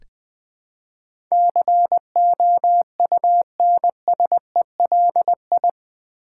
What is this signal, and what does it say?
Morse code, heard as a keyed tone, about 20 wpm, 700 Hz. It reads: COUNSELI
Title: Build A Word - 4 Letters At A Time - 10-Letter Words - 20wpm